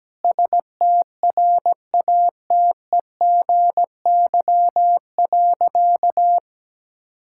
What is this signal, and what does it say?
Morse code, heard as a keyed tone, about 17 wpm, 700 Hz. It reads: STRATEGY.
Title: Sentences from Top 1000 Words - Review Part 5 - 17wpm